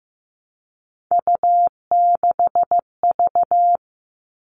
Morse code, heard as a keyed tone, about 15 wpm, 700 Hz. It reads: U6V
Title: All Cut Numbers - 3-Digit 15wpm